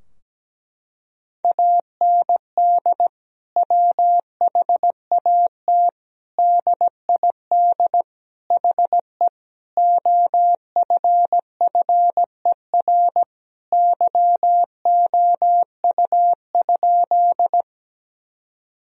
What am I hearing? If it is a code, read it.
Morse code, heard as a keyed tone, about 17 wpm, 700 Hz. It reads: AND WHAT DID HE OFFER YOU?
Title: Sentences from Top 1000 Words - Review Part 4 - 17wpm